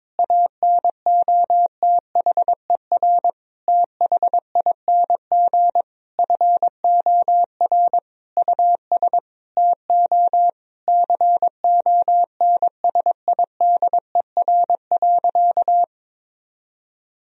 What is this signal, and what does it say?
Morse code, heard as a keyed tone, about 22 wpm, 700 Hz. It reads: ANOTHER THING FOR US TO CONSIDER.
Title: Sentences from Top 1000 Words - Review Part 4 - 22wpm